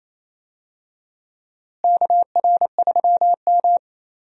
Morse code, heard as a keyed tone, about 28 wpm, 700 Hz. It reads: KR3M
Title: USA Call Signs - 4 Characters - Repeated 3x - 28wpm